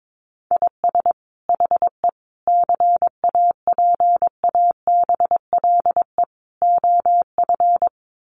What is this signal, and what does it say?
Morse code, heard as a keyed tone, about 22 wpm, 700 Hz. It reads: IS HE CAPABLE OF
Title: Sentences from Top 1200 Words 22wpm